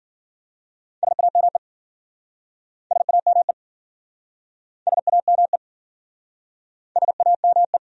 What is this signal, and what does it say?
Morse code, heard as a keyed tone, about 52 wpm, 700 Hz. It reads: SAME SAME SAME SAME